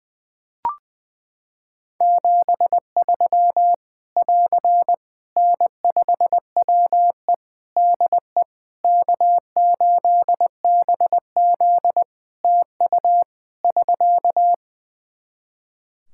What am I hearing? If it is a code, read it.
Morse code, heard as a keyed tone, about 20 wpm, 700 Hz. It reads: 73 <AR> N5WE DE K8BZ TU <SK>
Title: K1USN SST QSO Practice 20wpm